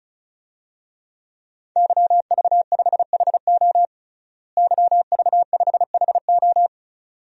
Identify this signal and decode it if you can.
Morse code, heard as a keyed tone, about 36 wpm, 700 Hz. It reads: YV5HO YV5HO
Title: DXCC-100 Call Signs - 5 Characters - Repeated 2x - 35wpm